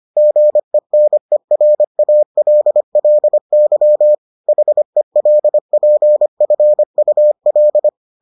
Morse code, heard as a keyed tone, about 25 wpm, 600 Hz. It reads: GENERALLY HELPFUL